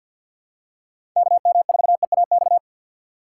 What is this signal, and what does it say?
Morse code, heard as a keyed tone, about 48 wpm, 700 Hz. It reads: KM4EAX
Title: USA Call Signs - 6 Characters - 6x Character Spacing - 50wpm